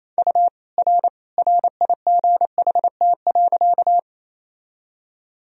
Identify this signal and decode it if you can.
Morse code, heard as a keyed tone, about 28 wpm, 700 Hz. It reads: U R RIGHT.
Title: Sentences from Top 200 Words and Abbreviations 28wpm